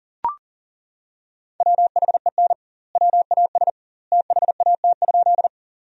Morse code, heard as a keyed tone, about 40 wpm, 700 Hz. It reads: WHEN WAS THAT?
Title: 2x Word-Spacing - Sentences from Top 100 Words - 40wpm